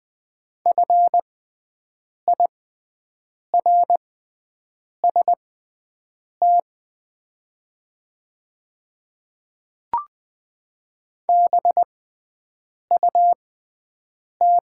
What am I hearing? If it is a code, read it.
Morse code, heard as a keyed tone, about 20 wpm, spaced as for 7 wpm, 700 Hz. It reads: FIRST BUT
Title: 6x Character Spacing - Top 100 Words - 20wpm